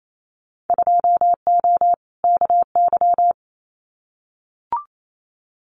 Morse code, heard as a keyed tone, about 28 wpm, 700 Hz. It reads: 2O KY
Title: Winter Field Day Exchanges 28wpm